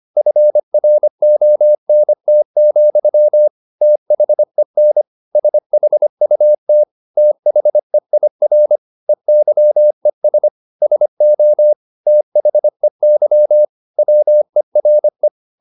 Morse code, heard as a keyed tone, about 25 wpm, 600 Hz. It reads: FRONT, THEN SHUT THEIR EYES SO THEY WERE
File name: the old man and the sea part 1 25wpm